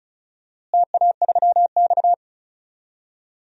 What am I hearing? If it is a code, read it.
Morse code, heard as a keyed tone, about 36 wpm, 700 Hz. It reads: TA3X
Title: DXCC-100 Call Signs - 4 Characters - Repeated 3x - 35wpm